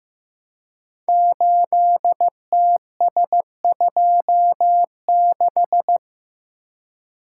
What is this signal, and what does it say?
Morse code, heard as a keyed tone, about 15 wpm, 700 Hz. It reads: 8TS26
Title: Code Groups - 5 Letter:Numbers - 15wpm